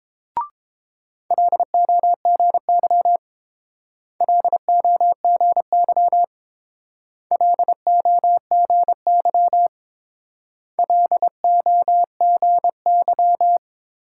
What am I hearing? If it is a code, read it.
Morse code, heard as a keyed tone, about 29 wpm, 690 Hz. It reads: LOGY LOGY LOGY LOGY